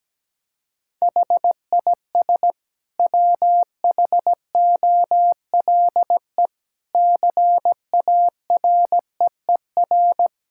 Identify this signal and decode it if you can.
Morse code, heard as a keyed tone, about 17 wpm, 700 Hz. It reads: HIS WHOLE CAREER
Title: Sentences from Top 1000 Words - Review Part 6 - 17wpm